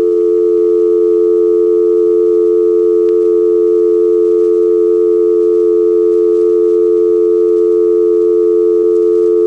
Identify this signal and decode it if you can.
Dial tone (North America), 350 + 440 Hz.